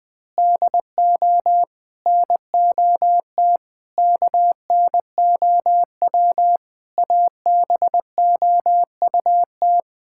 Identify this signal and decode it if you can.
Morse code, heard as a keyed tone, about 20 wpm, 700 Hz. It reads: DO NOT KNOW ABOUT
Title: Sentences with 5-Word Phrases - 20wpm